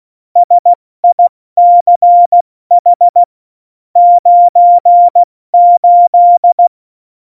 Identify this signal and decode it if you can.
Morse code, heard as a keyed tone, about 16 wpm, spaced as for 14 wpm, 700 Hz. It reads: SICH 98